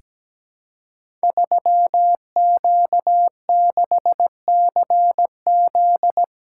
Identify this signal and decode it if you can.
Morse code, heard as a keyed tone, about 17 wpm, 700 Hz. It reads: 3Q6CZ
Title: Code Groups - 5 Letter:Numbers - 17wpm